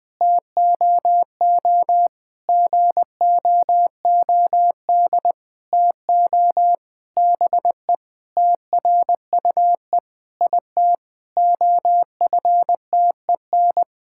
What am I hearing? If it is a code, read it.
Morse code, heard as a keyed tone, about 20 wpm, 700 Hz. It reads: TOO GOOD TO BE TRUE IT OFTEN
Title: Sentences with 9-Word Phrases - 20wpm